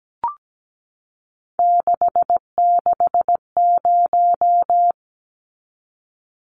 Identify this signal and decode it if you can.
Morse code, heard as a keyed tone, about 17 wpm, 700 Hz. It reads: 660
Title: Numbers - 3-Digit 17wpm